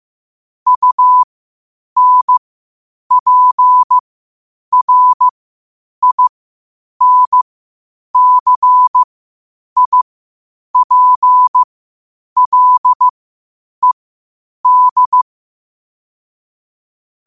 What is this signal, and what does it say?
Morse code, heard as a keyed tone, about 15 wpm, spaced as for 8 wpm, 1000 Hz. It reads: UNPRINCIPLED
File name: qotdfiles / 8wpm